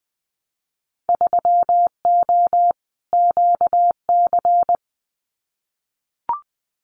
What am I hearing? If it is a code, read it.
Morse code, heard as a keyed tone, about 20 wpm, 700 Hz. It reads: 3O QC
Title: Winter Field Day Exchanges Encore 20wpm